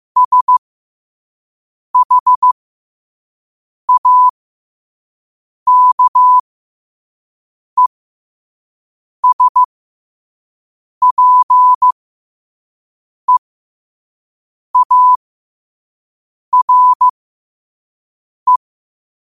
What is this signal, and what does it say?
Morse code, heard as a keyed tone, about 15 wpm, spaced as for 5 wpm, 1000 Hz. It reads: SHAKESPEARE
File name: qotdfiles / 5wpm